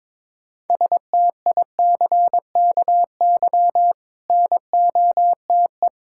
Morse code, heard as a keyed tone, about 22 wpm, 690 Hz. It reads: STICKY NOTE
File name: Difficult Words with Hints - 22wpm